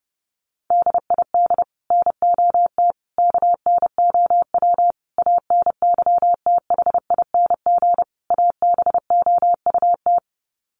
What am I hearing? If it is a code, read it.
Morse code, heard as a keyed tone, about 30 wpm, 700 Hz. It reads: DID NOT KNOW ANYTHING ABOUT
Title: Sets of 5 Words Encore2 30wpm